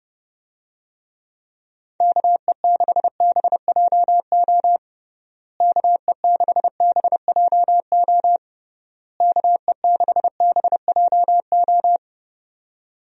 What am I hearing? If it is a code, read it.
Morse code, heard as a keyed tone, about 32 wpm, 700 Hz. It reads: KE6BJO KE6BJO KE6BJO
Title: USA Call Signs - 6 Characters - Repeated 3x - 30wpm